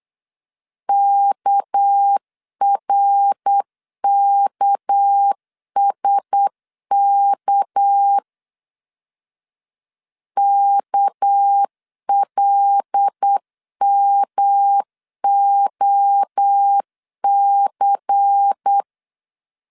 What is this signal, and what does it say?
Morse code, heard as a keyed tone, about 8 wpm, 800 Hz. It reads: KRKSK KLMOC
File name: LEZIONI / 6 PKX0 / 6 PK 8wpm 3003